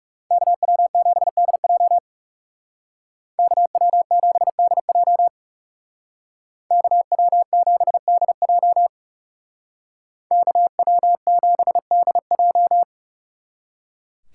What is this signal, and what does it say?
Morse code, heard as a keyed tone, about 39 wpm, 700 Hz. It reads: KW7DJ KW7DJ KW7DJ KW7DJ